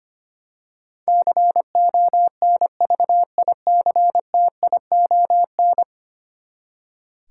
Morse code, heard as a keyed tone, about 25 wpm, 700 Hz. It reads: CONVICTION
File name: Ten Letter Words - 25wpm